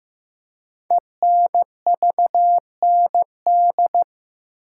Morse code, heard as a keyed tone, about 15 wpm, 700 Hz. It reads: ENVND
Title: Code Groups - 5 Letters - 15wpm